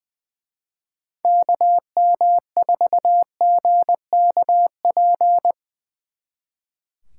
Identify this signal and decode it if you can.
Morse code, heard as a keyed tone, about 22 wpm, 700 Hz. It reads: KM4GKP
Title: Speed Racing - USA Call Signs 20wpm